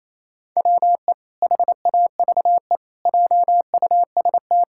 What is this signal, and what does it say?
Morse code, heard as a keyed tone, about 28 wpm, 700 Hz. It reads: WE HAVE JUST
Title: Pangram Challenge v2 - 28wpm